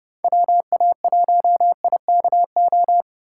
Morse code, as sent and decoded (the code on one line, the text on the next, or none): .-- .- .---- .. -.- ---
WA1IKO